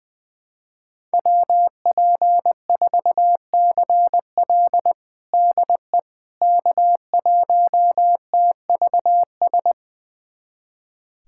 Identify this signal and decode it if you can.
Morse code, heard as a keyed tone, about 20 wpm, 700 Hz. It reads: WP4CL DE K1TVS
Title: USA Callsign DE Callsign - 5 Characters - 20wpm